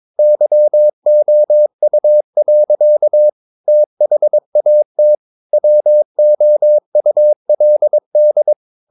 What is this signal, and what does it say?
Morse code, heard as a keyed tone, about 22 wpm, 600 Hz. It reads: YOU. THAT WOULD
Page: Romanen Never af Ken Follet på engelsk.